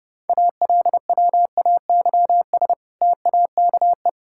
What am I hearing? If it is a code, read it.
Morse code, heard as a keyed tone, about 32 wpm, 700 Hz. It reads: ALWAYS TAKE
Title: Sentences from Top 1000 Words - Review Part 5 - 30wpm